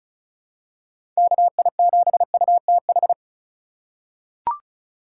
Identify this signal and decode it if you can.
Morse code, heard as a keyed tone, about 36 wpm, 700 Hz. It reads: KI7UTH
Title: USA Call Signs - 6 Characters - 3x Character Spacing - 35wpm